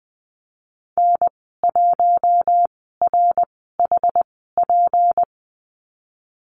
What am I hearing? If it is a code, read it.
Morse code, heard as a keyed tone, about 20 wpm, 700 Hz. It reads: N 1 R H P
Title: 238 Lesson 38g - Letter q - Call Signs 20wpm